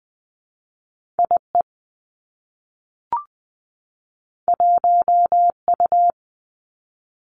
Morse code, heard as a keyed tone, about 20 wpm, 700 Hz. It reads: I E 1 U
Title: Two Challenging Letter-Number Combinations 20wpm